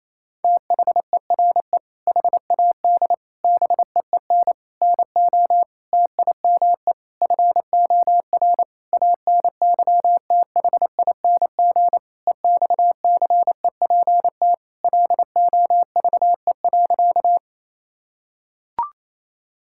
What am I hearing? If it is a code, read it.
Morse code, heard as a keyed tone, about 28 wpm, 700 Hz. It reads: THERE HAD BEEN NO TIME FOR ANYTHING EXCEPT LOVE.
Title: Sentences from Top 1600 Words 28wpm